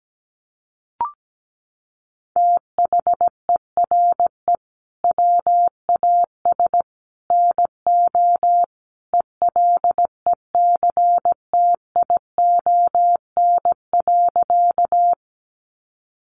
Morse code, heard as a keyed tone, about 17 wpm, 700 Hz. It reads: THERE WAS NO ELECTION.
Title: Sentences from Top 1500 Words 17wpm